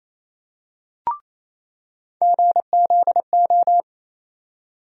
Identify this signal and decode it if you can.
Morse code, heard as a keyed tone, about 28 wpm, 700 Hz. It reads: GZO
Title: USA Call Sign Postfixes - 3 Letters 28wpm